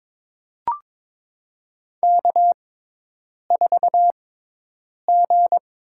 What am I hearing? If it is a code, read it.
Morse code, heard as a keyed tone, about 22 wpm, spaced as for 8 wpm, 700 Hz. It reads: K4G
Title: USA Call Signs - 5 Characters - 6x Character Spacing - 22wpm